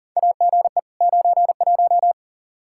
Morse code, heard as a keyed tone, about 40 wpm, 700 Hz. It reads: AGE 91
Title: Practice QSO - Exchange 3 - More Info - 40wpm